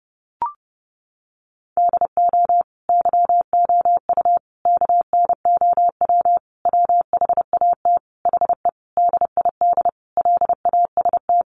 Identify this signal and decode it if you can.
Morse code, heard as a keyed tone, about 30 wpm, 700 Hz. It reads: DO YOU KNOW WHAT HE DID LAST
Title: Sentences from Top 300 Words 30wpm